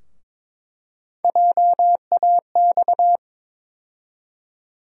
Morse code, heard as a keyed tone, about 22 wpm, 700 Hz. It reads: JAX
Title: USA Call Sign Postfixes - 3 Letters 22wpm